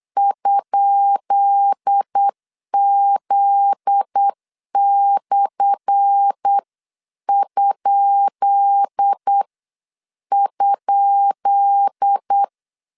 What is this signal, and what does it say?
Morse code, heard as a keyed tone, about 8 wpm, spaced as for 7 wpm, 800 Hz. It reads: ?Z/??
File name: LEZIONI / 10 SIGNS / 10B INT 8wpm 3534